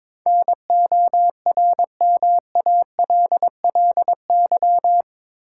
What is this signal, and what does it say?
Morse code, heard as a keyed tone, about 22 wpm, 700 Hz. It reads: NORMALLY
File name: practice / 22wpm / Eight Letter Words - 22wpm